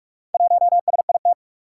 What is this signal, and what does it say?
Morse code, heard as a keyed tone, about 44 wpm, 700 Hz. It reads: 1SIT
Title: Four Challenging Letter-Number Combinations 45wpm